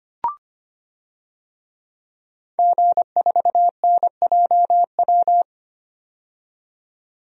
Morse code, heard as a keyed tone, about 25 wpm, 700 Hz. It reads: G4NJW
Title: DXCC-100 Call Signs - 5 Characters - 25wpm